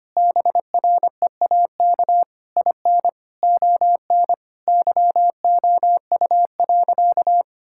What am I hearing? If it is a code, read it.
Morse code, heard as a keyed tone, about 25 wpm, 700 Hz. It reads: BREAK IN ON YOU.
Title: Sentences from Top 1000 Words - Review Part 6 - 25wpm